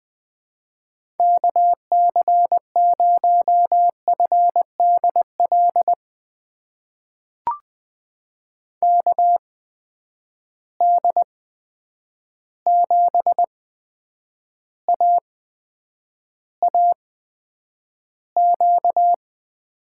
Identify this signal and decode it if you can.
Morse code, heard as a keyed tone, about 20 wpm, spaced as for 9 wpm, 700 Hz. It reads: KC0FDL KD7AAQ
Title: USA Call Signs - 6 Characters - 8x Character Spacing - 20wpm